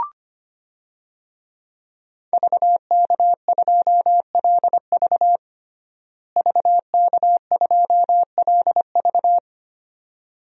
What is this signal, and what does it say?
Morse code, heard as a keyed tone, about 25 wpm, 700 Hz. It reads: VK2LV VK2LV
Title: DXCC-100 Call Signs - 5 Characters - Repeated 2x - 25wpm